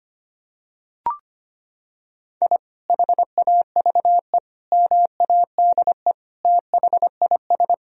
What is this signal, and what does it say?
Morse code, heard as a keyed tone, about 25 wpm, 700 Hz. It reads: I HAVE MADE THIS
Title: Sentences from Top 1000 Words - Review Part 2 - 25wpm